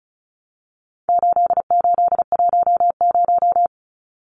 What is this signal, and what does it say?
Morse code, heard as a keyed tone, about 34 wpm, 700 Hz. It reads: 8810
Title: Numbers - 4-Digit 35wpm